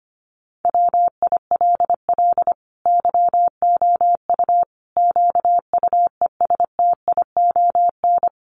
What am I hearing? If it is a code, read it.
Morse code, heard as a keyed tone, about 25 wpm, 700 Hz. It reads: WILL YOU QUESTION
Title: Sentences from Top 1100 Words 25wpm